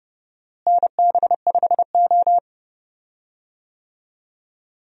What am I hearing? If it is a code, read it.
Morse code, heard as a keyed tone, about 30 wpm, 700 Hz. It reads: NB5O
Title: USA Call Signs - 4 Characters - Repeated 3x - 30wpm